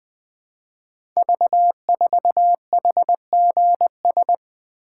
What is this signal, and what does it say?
Morse code, heard as a keyed tone, about 20 wpm, 700 Hz. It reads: V4HGS
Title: Code Groups - 5 Letter:Numbers - 20wpm